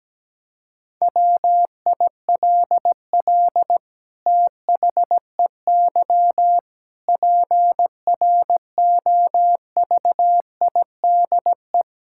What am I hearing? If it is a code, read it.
Morse code, heard as a keyed tone, about 17 wpm, 700 Hz. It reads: WILL THEY PROVIDE
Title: Sentences from Top 1000 Words - Review Part 5 - 17wpm